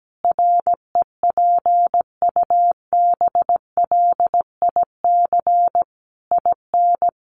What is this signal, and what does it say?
Morse code, heard as a keyed tone, about 17 wpm, 700 Hz. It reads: REPUBLIC IN
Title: Sentences from Top 1500 Words 17wpm